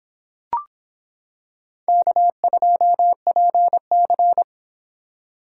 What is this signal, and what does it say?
Morse code, heard as a keyed tone, about 26 wpm, 700 Hz. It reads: K2PC